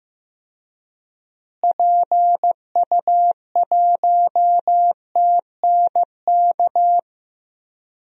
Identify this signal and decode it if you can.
Morse code, heard as a keyed tone, about 16 wpm, 700 Hz. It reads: PU1TNK
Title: DXCC-100 Call Signs - 6 Characters - Repeated 3x - 15wpm